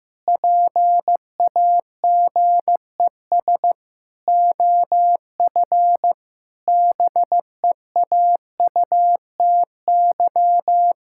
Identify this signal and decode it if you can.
Morse code, heard as a keyed tone, about 15 wpm, 700 Hz. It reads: PAGES OF BEAUTY
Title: Pangram Challenge v2 - 15wpm